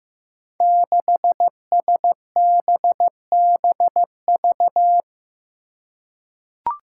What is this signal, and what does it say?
Morse code, heard as a keyed tone, about 15 wpm, 700 Hz. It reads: 6SBBV
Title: Five Challenging Letter-Number Combinations 15wpm